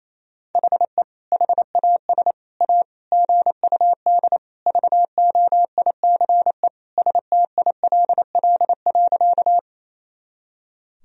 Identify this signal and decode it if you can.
Morse code, heard as a keyed tone, about 28 wpm, 700 Hz. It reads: HE HAS A GUD VOICE STILL.
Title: Sentences from Top 500 Words and Abbreviations 28wpm